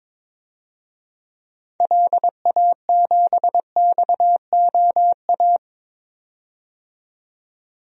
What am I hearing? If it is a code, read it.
Morse code, heard as a keyed tone, about 22 wpm, 700 Hz. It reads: LA7XOA
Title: DXCC-100 Call Signs - 6 Characters - 22wpm